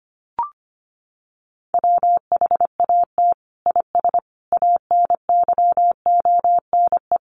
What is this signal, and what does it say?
Morse code, heard as a keyed tone, about 25 wpm, 700 Hz. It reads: WHAT IS ANYONE
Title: Sentences from Top 900 Words 25wpm